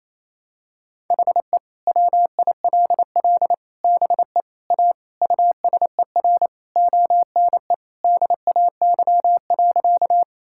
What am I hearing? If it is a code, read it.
Morse code, heard as a keyed tone, about 28 wpm, 700 Hz. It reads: HE WILL BE A USER ONE DAY.
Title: Sentences from Top 1000 Words - Review Part 5 - 28wpm